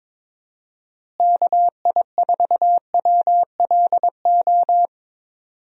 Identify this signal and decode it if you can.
Morse code, heard as a keyed tone, about 22 wpm, 700 Hz. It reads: KI4WLO
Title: USA Call Signs - 6 Characters - Repeated 2x - 22wpm